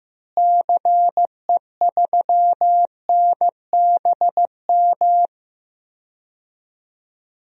Morse code, heard as a keyed tone, about 15 wpm, 700 Hz. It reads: CE3NBM
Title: DXCC-100 Call Signs - 6 Characters - Repeated 3x - 15wpm